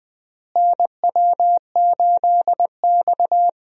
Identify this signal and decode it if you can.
Morse code, heard as a keyed tone, about 21 wpm, 700 Hz. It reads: NW8X
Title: USA Call Signs - 4 Characters - Repeated 3x - 20wpm